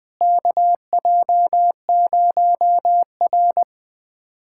Morse code, heard as a keyed tone, about 20 wpm, 700 Hz. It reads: KJ0R
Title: USA Callsign DE Callsign - 4 Characters - 20wpm